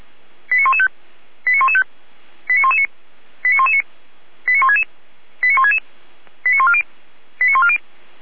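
Five-tone selective call (ZVEI), heard as quick runs of five tones, tones of about 70 ms. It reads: 89115 89115 89119 89119 89166 89166 89140 89140
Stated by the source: ZVEI five-tone call received on SDR.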